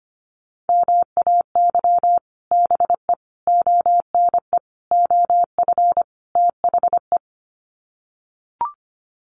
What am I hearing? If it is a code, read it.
Morse code, heard as a keyed tone, about 25 wpm, 700 Hz. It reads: MAY BE ONE OF THE
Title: Sets of 5 Words Encore2 25wpm